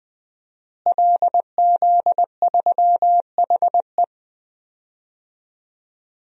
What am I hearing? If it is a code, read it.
Morse code, heard as a keyed tone, about 20 wpm, 700 Hz. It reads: LZ3HE
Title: DXCC-100 Call Signs - 5 Characters - Repeated 2x - 20wpm